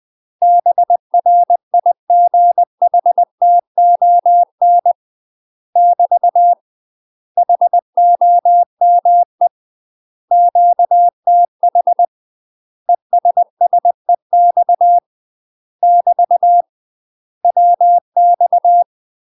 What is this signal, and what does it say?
Morse code, heard as a keyed tone, about 20 wpm, 700 Hz. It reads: BRIGHTON <BT> HOME QTH ESSEX <BT> WX
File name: UK MASTER QSOs TRAINING / 20WPM / UK MASTER QSOs 0015